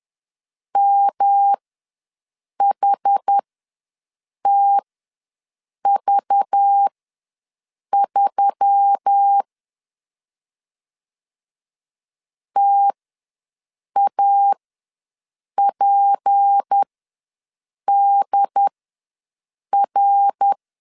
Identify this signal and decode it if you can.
Morse code, heard as a keyed tone, about 10 wpm, spaced as for 6 wpm, 800 Hz. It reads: MHTV3 TAPDR
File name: LEZIONI / 6 PKX0 / 6 P ws 10wpm 3751